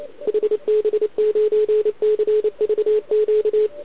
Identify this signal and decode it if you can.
Morse code, heard as a keyed tone, about 31 wpm, 470 Hz, with radio noise. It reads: HB9CVQ